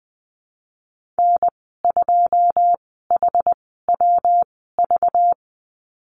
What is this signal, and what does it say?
Morse code, heard as a keyed tone, about 20 wpm, spaced as for 15 wpm, 700 Hz. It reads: N2HWV
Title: 207 Lesson 33g - Letter j - Call Signs 20wpm